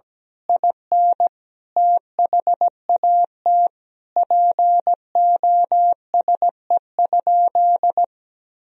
Morse code, heard as a keyed tone, about 17 wpm, 700 Hz. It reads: IN THAT POSE?
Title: Sentences from Top 1000 Words - Review Part 2 - 17wpm